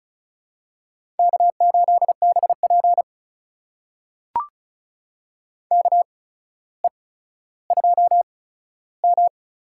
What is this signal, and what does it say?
Morse code, heard as a keyed tone, about 34 wpm, spaced as for 15 wpm, 700 Hz. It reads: K8BP KE2M
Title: USA Call Signs - 8x Character Spacing - 35wpm